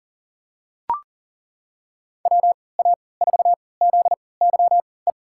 Morse code, heard as a keyed tone, about 40 wpm, spaced as for 23 wpm, 700 Hz. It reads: WA4ZYE
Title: USA Call Signs - 6 Characters - 3x Character Spacing - 40wpm